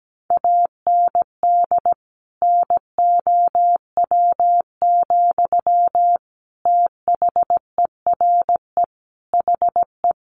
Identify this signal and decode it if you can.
Morse code, heard as a keyed tone, about 17 wpm, 700 Hz. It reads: AND NOW, THERE HE
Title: Sentences from Top 1500 Words 17wpm